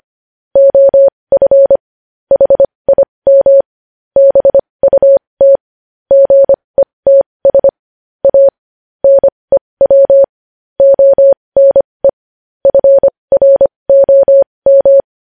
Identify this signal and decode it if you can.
Morse code, heard as a keyed tone, about 25 wpm, spaced as for 20 wpm, 550 Hz. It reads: OF HIM BUT GETS A NEW ONE FROM
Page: news-headlines-in-morse-code-at-20-wpm